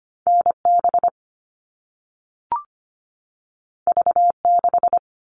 Morse code, heard as a keyed tone, about 25 wpm, 700 Hz. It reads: NB V6
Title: All Cut Numbers - 2-Digit 25wpm